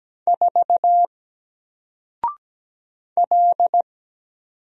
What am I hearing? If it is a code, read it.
Morse code, heard as a keyed tone, about 17 wpm, 700 Hz. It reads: 4 L
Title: Single Character (Letter:Number:Punctuation:Prosign) Alt 17wpm